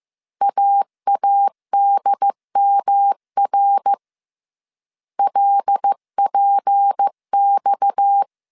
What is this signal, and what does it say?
Morse code, heard as a keyed tone, about 15 wpm, 800 Hz. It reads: AADMR LPX
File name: CW / DET / 1 6 1500C S C 14wpm 2951